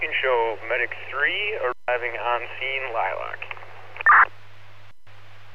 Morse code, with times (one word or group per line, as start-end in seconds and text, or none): none